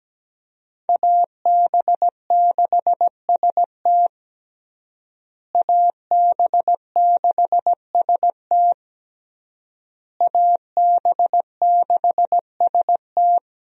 Morse code, heard as a keyed tone, about 17 wpm, 700 Hz. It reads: AB6ST AB6ST AB6ST
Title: USA Call Signs - 5 Characters - Repeated 3x - 17wpm